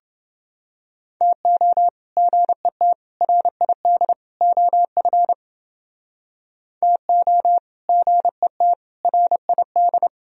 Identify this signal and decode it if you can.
Morse code, heard as a keyed tone, about 30 wpm, 700 Hz. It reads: TO GET RID OF TO GET RID